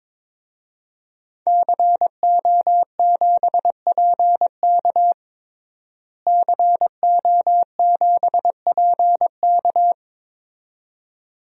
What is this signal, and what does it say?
Morse code, heard as a keyed tone, about 22 wpm, 700 Hz. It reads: CO7PK CO7PK
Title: DXCC-100 Call Signs - 5 Characters - Repeated 2x - 22wpm